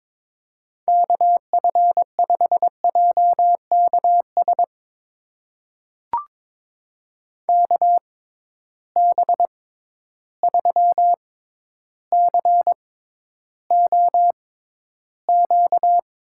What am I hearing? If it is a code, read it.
Morse code, heard as a keyed tone, about 22 wpm, spaced as for 11 wpm, 700 Hz. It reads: KF5JKS KB3COQ
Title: USA Call Signs - 6 Characters - 6x Character Spacing - 22wpm